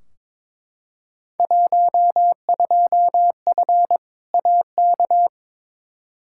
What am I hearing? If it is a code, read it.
Morse code, heard as a keyed tone, about 22 wpm, 700 Hz. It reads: 12F AK
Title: Field Day Exchanges Encore - 22wpm